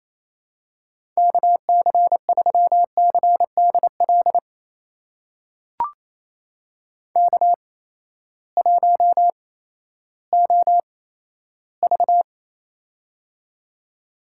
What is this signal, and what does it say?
Morse code, heard as a keyed tone, about 28 wpm, spaced as for 14 wpm, 700 Hz. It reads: KC3CDL K1OV
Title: USA Call Signs - 8x Character Spacing - 28wpm